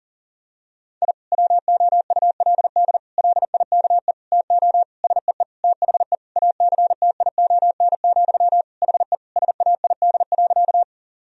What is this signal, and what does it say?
Morse code, heard as a keyed tone, about 37 wpm, 700 Hz. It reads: I WOULD LIKE TO SEE THE ACTION, HE SAID.
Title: Sentences from Top 1000 Words - Review Part 4 - 40wpm